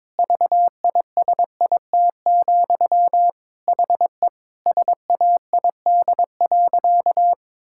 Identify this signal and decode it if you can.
Morse code, heard as a keyed tone, about 22 wpm, 700 Hz. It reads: VISIT, HE SAID.
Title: Sentences from Top 1000 Words - Review Part 6 - 22wpm